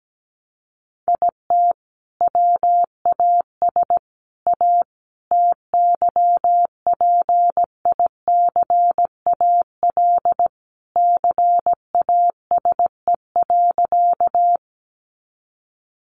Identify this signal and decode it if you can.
Morse code, heard as a keyed tone, about 17 wpm, 700 Hz. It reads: IT WAS A TYPICAL CASE.
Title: Sentences from Top 1100 Words 17wpm